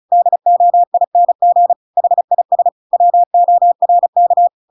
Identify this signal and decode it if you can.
Morse code, heard as a keyed tone, about 34 wpm, 700 Hz. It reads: DOING HIS WORK
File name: Pinocchio / Pinocchio ch8 35 35 ik1hge